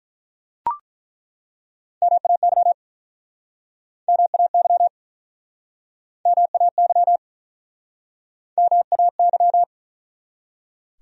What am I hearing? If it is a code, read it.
Morse code, heard as a keyed tone, about 47 wpm, 700 Hz. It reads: MAY MAY MAY MAY